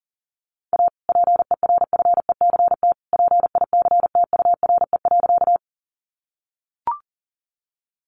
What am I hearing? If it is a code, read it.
Morse code, heard as a keyed tone, about 40 wpm, 700 Hz. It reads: A PERFECT PICTURE.
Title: Sentences from Top 1000 Words 40wpm